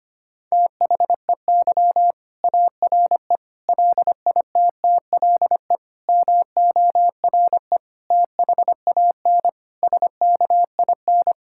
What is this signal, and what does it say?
Morse code, heard as a keyed tone, about 25 wpm, 700 Hz. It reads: THEY ARE LITTLE MORE THAN SKIN
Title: Sentences from Top 1600 Words 25wpm